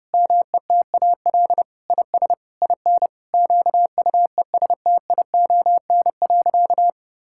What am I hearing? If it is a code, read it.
Morse code, heard as a keyed tone, about 30 wpm, 700 Hz. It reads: METAL IS IN QUESTION.
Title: Sentences from Top 1000 Words - Review Part 6 - 30wpm